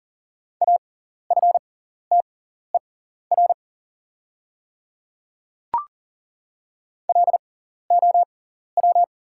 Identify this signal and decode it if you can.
Morse code, heard as a keyed tone, about 40 wpm, spaced as for 14 wpm, 700 Hz. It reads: AFTER LOW
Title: 6x Character Spacing - Top 200 Words - 40wpm